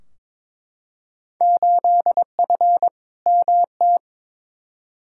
Morse code, heard as a keyed tone, about 22 wpm, 700 Hz. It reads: 8F MT
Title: Field Day Exchanges Encore - 22wpm